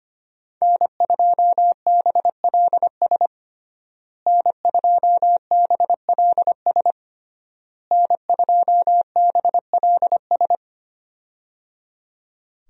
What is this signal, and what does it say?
Morse code, heard as a keyed tone, about 25 wpm, 700 Hz. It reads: N2BLS N2BLS N2BLS
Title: USA Call Signs - 5 Characters - Repeated 3x - 25wpm